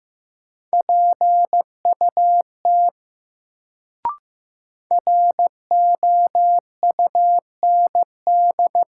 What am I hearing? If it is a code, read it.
Morse code, heard as a keyed tone, about 15 wpm, 700 Hz. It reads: PUT ROUND
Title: Sentences from Top 200 Words - New Word Drill - 15wpm